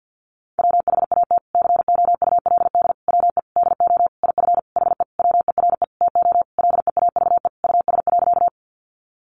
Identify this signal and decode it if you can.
Morse code, heard as a keyed tone, about 50 wpm, 700 Hz. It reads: WHAT COULD WE DO IF HE WERE TO LEAVE US.
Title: Sentences from Top 400 Words 50wpm